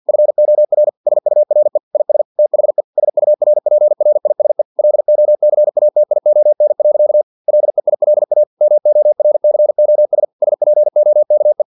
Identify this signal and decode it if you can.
Morse code, heard as a keyed tone, about 50 wpm, 600 Hz. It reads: FOR SURE IS THE SURPRISE LOCATION. LEILA MORCOS SPOKE
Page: Never 49wpm